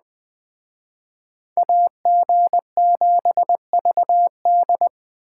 Morse code, heard as a keyed tone, about 21 wpm, 700 Hz. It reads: AG7VD